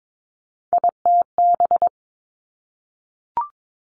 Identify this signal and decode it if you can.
Morse code, heard as a keyed tone, about 22 wpm, 700 Hz. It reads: ITB
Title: Three Challenging Letter-Number Combinations 22wpm